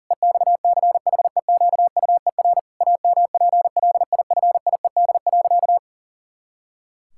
Morse code, heard as a keyed tone, about 40 wpm, 700 Hz. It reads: EXCHEQUER AMPLIFIED.
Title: Pangram Challenge v2 - 40wpm